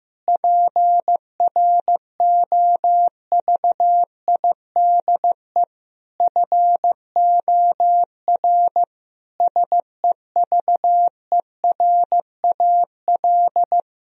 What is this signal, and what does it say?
Morse code, heard as a keyed tone, about 15 wpm, 700 Hz. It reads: PROVIDE FOR SEVERAL
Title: Sentences from Top 1000 Words - Review Part 2 - 15wpm